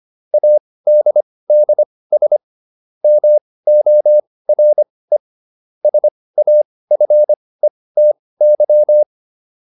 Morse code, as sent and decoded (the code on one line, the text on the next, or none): .- -.. -.. ... / -- --- .-. / ... .- ..-. - -.--
ADDS MORE SAFETY